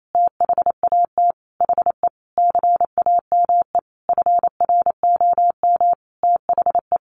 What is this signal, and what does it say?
Morse code, heard as a keyed tone, about 28 wpm, 700 Hz. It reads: THAT HE CAME FROM THE
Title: Sentences from Top 1500 Words 28wpm